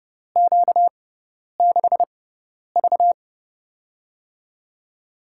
Morse code, heard as a keyed tone, about 29 wpm, spaced as for 10 wpm, 700 Hz. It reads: Q6V
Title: USA Call Signs - 4 Characters - 6x Character Spacing - 30wpm